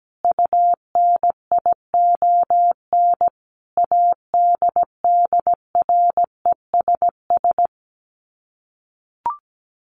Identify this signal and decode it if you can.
Morse code, heard as a keyed tone, about 17 wpm, 700 Hz. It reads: UNION ADDRESS
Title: Sets of 5 Words Encore 17wpm